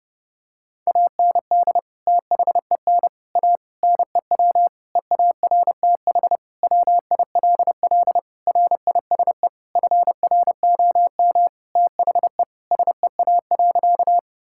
Morse code, heard as a keyed tone, about 30 wpm, 700 Hz. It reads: AND THEN A NEW EARTH WILL RISE FROM THE SEA.
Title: Sentences from Top 1000 Words - Review Part 6 - 30wpm